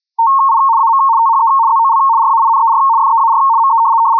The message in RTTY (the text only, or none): I AM THE ZOL RAINER FEAR ME